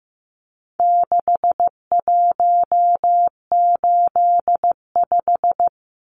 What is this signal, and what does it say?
Morse code, heard as a keyed tone, about 15 wpm, 700 Hz. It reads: 6185
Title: Numbers - 4-Digit 15wpm